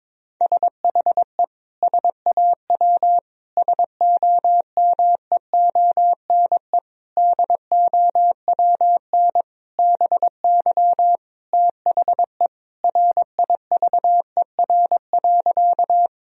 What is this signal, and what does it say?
Morse code, heard as a keyed tone, about 21 wpm, 700 Hz. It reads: SHE SAW SOMEONE DOWN BY THE RIVER.
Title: Sentences from Top 1000 Words - Review Part 4 - 22wpm